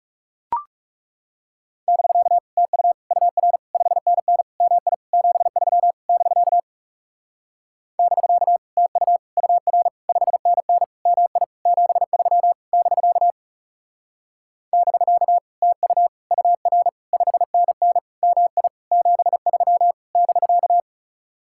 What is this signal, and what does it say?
Morse code, heard as a keyed tone, about 37 wpm, 700 Hz. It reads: <BK> TU UR 5NN MI 73 <BK> <BK> TU UR 5NN MI 73 <BK> <BK> TU UR 5NN MI 73 <BK>
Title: Speed-Racing Common QSO Phrases - 30wpm